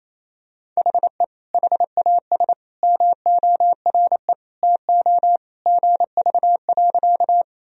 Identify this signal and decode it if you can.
Morse code, heard as a keyed tone, about 28 wpm, 700 Hz. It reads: HE HAS MORE TO GV.
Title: Sentences from Top 200 Words and Abbreviations 28wpm